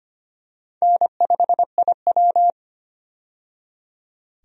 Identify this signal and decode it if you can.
Morse code, heard as a keyed tone, about 25 wpm, 700 Hz. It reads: N5IW
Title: USA Call Signs - 4 Characters - 25wpm